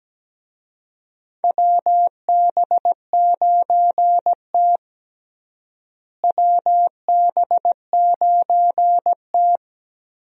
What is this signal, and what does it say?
Morse code, heard as a keyed tone, about 17 wpm, 700 Hz. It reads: WB9T WB9T
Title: USA Call Signs - 4 Characters - Repeated 2x - 17wpm